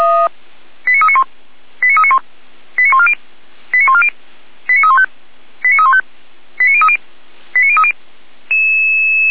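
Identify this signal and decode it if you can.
Five-tone selective call (ZVEI), heard as quick runs of five tones, tones of about 70 ms. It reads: 89391 89391 89140 89140 89315 89315 89030 89030
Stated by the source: ZVEI five-tone call 89042 received on SDR.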